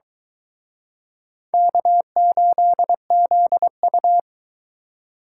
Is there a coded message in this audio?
Yes: Morse code from 1.5 to 4.2 s, heard as a keyed tone, about 23 wpm, 700 Hz. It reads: K8ZU